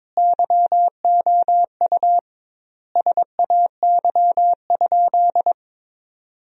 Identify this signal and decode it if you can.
Morse code, heard as a keyed tone, about 21 wpm, 700 Hz. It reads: YOU SAY?
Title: 2x Word-Spacing - Sentences from Top 200 Words - 22wpm